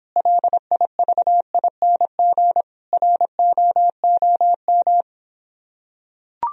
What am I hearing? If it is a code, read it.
Morse code, heard as a keyed tone, about 26 wpm, 700 Hz. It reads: LIVING ROOM